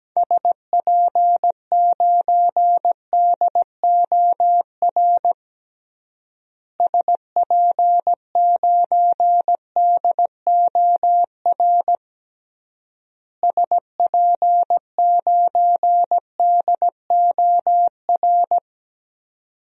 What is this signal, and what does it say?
Morse code, heard as a keyed tone, about 17 wpm, 700 Hz. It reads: SP9DOR SP9DOR SP9DOR
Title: DXCC-100 Call Signs - 6 Characters - Repeated 3x - 17wpm